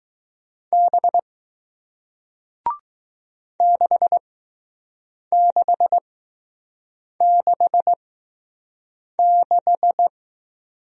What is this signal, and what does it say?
Morse code, heard as a keyed tone, about 20 wpm, spaced as for 6 wpm, 700 Hz. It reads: B 6666